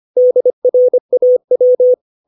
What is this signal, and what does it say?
Morse code, heard as a keyed tone, about 25 wpm, 500 Hz. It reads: DRAW
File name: 632 words / 632 Words 25WPM